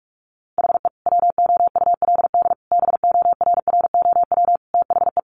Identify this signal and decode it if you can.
Morse code, heard as a keyed tone, about 44 wpm, 700 Hz. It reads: HE WOULD BORROW THE
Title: Sentences from Top 1500 Words 45wpm